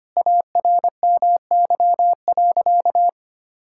Morse code, heard as a keyed tone, about 25 wpm, 700 Hz. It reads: ARMY.
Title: Sentences from Top 1000 Words - Review Part 6 - 25wpm